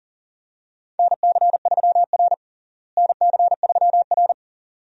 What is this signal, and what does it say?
Morse code, heard as a keyed tone, about 40 wpm, 700 Hz. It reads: NC3R NC3R
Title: USA Call Signs - 4 Characters - Repeated 2x - 40wpm